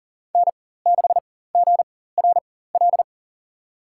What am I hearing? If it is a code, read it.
Morse code, heard as a keyed tone, about 40 wpm, spaced as for 19 wpm, 700 Hz. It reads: N6GRL
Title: USA Call Signs - 5 Characters - 4x Character Spacing - 40wpm